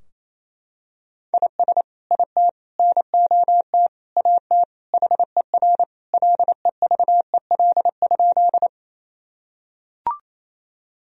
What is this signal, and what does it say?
Morse code, heard as a keyed tone, about 28 wpm, 700 Hz. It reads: IS IT NOT AT HER LEVEL?
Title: Sentences from Top 1000 Words - Review Part 4 - 28wpm